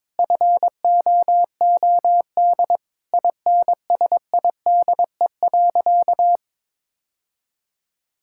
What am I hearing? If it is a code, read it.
Morse code, heard as a keyed tone, about 22 wpm, 700 Hz. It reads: FOOD INSIDE.
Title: Sentences from Top 1000 Words - Review Part 6 - 22wpm